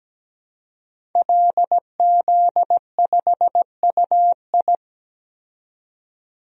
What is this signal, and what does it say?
Morse code, heard as a keyed tone, about 17 wpm, 700 Hz. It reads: LZ5UI
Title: DXCC-100 Call Signs - 5 Characters - 17wpm